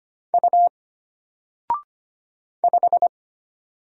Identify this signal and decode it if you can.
Morse code, heard as a keyed tone, about 25 wpm, 700 Hz. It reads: U 5
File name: Single-Letter-Number-25wpm